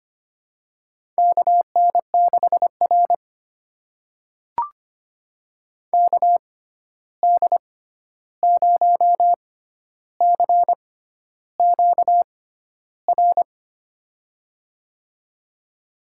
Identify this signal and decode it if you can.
Morse code, heard as a keyed tone, about 25 wpm, spaced as for 11 wpm, 700 Hz. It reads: KN6R KD0CQR
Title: USA Call Signs - 6x Character Spacing - 25wpm